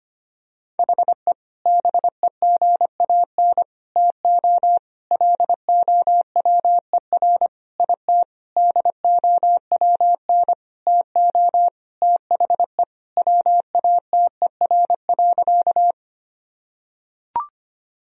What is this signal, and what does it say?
Morse code, heard as a keyed tone, about 25 wpm, 700 Hz. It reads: HE BEGAN TO LOWER IT DOWN TO THE WATER.
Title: Sentences from Top 1000 Words - Review Part 4 - 25wpm